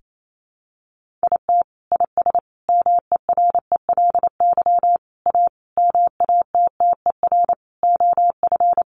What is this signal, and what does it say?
Morse code, heard as a keyed tone, about 28 wpm, 700 Hz. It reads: IT IS MERELY A MATTER OF
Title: Sentences from Top 1400 Words 28wpm